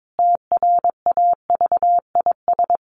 Morse code, heard as a keyed tone, about 22 wpm, 700 Hz. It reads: TRAVIS
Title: A Sound Of Thunder 22wpm